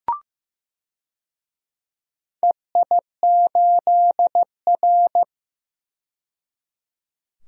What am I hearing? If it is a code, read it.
Morse code, heard as a keyed tone, about 15 wpm, 700 Hz. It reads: EI8R
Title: DXCC-100 Call Signs - 4 Characters - 15wpm